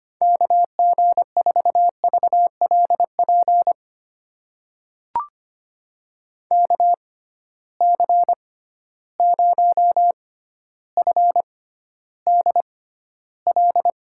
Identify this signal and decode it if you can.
Morse code, heard as a keyed tone, about 25 wpm, spaced as for 13 wpm, 700 Hz. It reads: KG4VLP KC0FDL
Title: USA Call Signs - 6 Characters - 6x Character Spacing - 25wpm